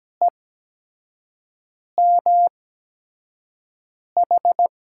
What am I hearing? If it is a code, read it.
Morse code, heard as a keyed tone, about 17 wpm, spaced as for 5 wpm, 700 Hz. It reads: EMH
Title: USA Call Signs - 6 Characters - 8x Character Spacing - 17wpm